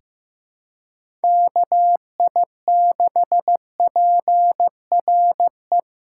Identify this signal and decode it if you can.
Morse code, heard as a keyed tone, about 15 wpm, 700 Hz. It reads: KI6PRE
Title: USA Call Signs - 4x Character Spacing - 15wpm